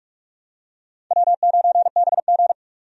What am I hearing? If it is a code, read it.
Morse code, heard as a keyed tone, about 43 wpm, 700 Hz. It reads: W9BG
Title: USA Call Signs - 4 Characters - 45wpm